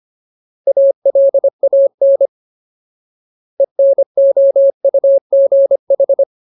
Morse code, heard as a keyed tone, about 25 wpm, 550 Hz. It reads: ALAN ENOUGH
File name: Ultimate Word List / 25WPM Ultimate